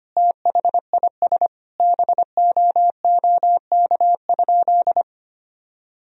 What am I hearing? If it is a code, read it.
Morse code, heard as a keyed tone, about 25 wpm, 700 Hz. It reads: THIS BOOK?
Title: Sentences from Top 1000 Words - Review Part 4 - 25wpm